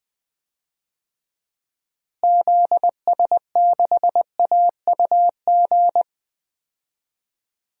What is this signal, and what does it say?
Morse code, heard as a keyed tone, about 20 wpm, 700 Hz. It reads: ZS6AUG
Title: DXCC-100 Call Signs - 6 Characters - Repeated 2x - 20wpm